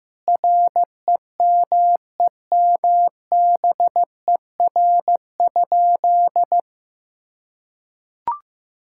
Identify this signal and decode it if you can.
Morse code, heard as a keyed tone, about 15 wpm, 700 Hz. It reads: REMEMBER?
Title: Sentences from Top 1000 Words - Review Part 6 - 15wpm